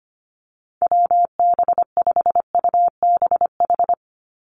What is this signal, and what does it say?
Morse code, heard as a keyed tone, about 25 wpm, 700 Hz. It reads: WB5UBH
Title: USA Call Signs 25wpm